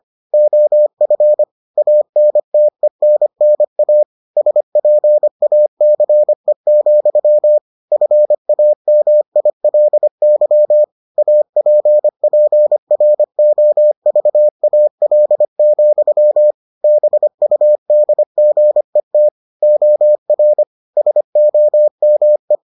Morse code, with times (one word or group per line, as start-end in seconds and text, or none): 0.3-1.4 OF
1.8-4.0 ANTENNA
4.4-7.6 SPACE,
7.9-10.8 FAMILY
11.2-16.5 APPROVAL,
16.8-19.3 BUDGET
19.6-20.6 OR
21.0-22.6 SOME